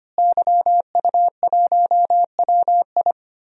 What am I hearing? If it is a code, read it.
Morse code, heard as a keyed tone, about 25 wpm, 700 Hz. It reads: YU1WI
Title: DXCC-100 Call Signs - 5 Characters - Repeated 2x - 25wpm